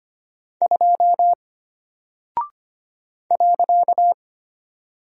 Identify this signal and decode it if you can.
Morse code, heard as a keyed tone, about 25 wpm, 700 Hz. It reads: 2 .
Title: Single Character (Letter:Number:Punctuation:Prosign) Alt 25wpm